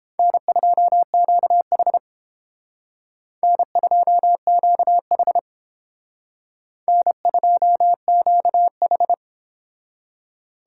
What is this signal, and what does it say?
Morse code, heard as a keyed tone, about 31 wpm, 700 Hz. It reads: N2QH N2QH N2QH